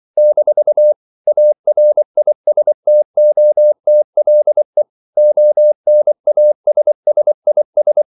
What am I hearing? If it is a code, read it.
Morse code, heard as a keyed tone, about 24 wpm, 600 Hz. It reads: - ARISTOTLE ONASSIS
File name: CW / Practice / EnglishQuotations0224